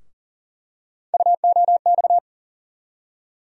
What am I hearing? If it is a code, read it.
Morse code, heard as a keyed tone, about 40 wpm, 700 Hz. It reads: UOX
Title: USA Call Sign Postfixes - 3 Letters 40wpm